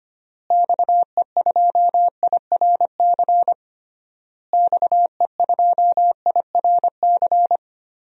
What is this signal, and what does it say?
Morse code, heard as a keyed tone, about 25 wpm, 700 Hz. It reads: XE2IRC XE2IRC
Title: DXCC-100 Call Signs - 6 Characters - Repeated 3x - 25wpm